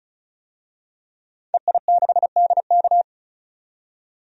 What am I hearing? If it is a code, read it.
Morse code, heard as a keyed tone, about 36 wpm, 700 Hz. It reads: EI6DK
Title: DXCC-100 Call Signs - 5 Characters - 35wpm